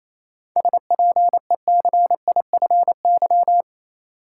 Morse code, heard as a keyed tone, about 28 wpm, 700 Hz. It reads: SPECIFY
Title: Seven Letter Words - 28wpm